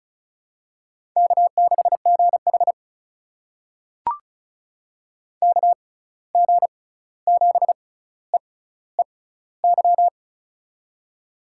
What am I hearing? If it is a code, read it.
Morse code, heard as a keyed tone, about 36 wpm, spaced as for 16 wpm, 700 Hz. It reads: K6GH KG7EEY
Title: USA Call Signs - 6x Character Spacing - 35wpm